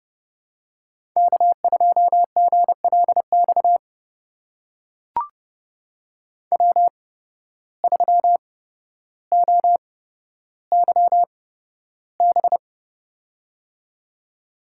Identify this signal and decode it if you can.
Morse code, heard as a keyed tone, about 30 wpm, spaced as for 13 wpm, 700 Hz. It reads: K2GLX W3OYB
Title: USA Call Signs - 5 Characters - 8x Character Spacing - 30wpm